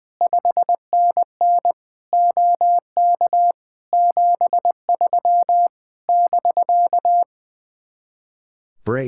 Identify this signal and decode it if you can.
Morse code, heard as a keyed tone, about 20 wpm, 700 Hz. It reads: 5NN OK 73 <BK>
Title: Speed-Racing Common QSO Phrases - 20wpm